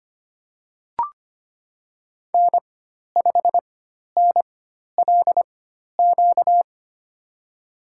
Morse code, heard as a keyed tone, about 25 wpm, spaced as for 12 wpm, 700 Hz. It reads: N5NLQ
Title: USA Call Signs - 5 Characters - 4x Character Spacing - 25wpm